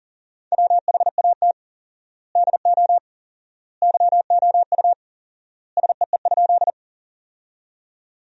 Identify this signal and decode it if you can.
Morse code, heard as a keyed tone, about 40 wpm, 700 Hz. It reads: WHAT DO YOU SEE?
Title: 4x Word-Spacing - Sentences from Top 100 Words - 40wpm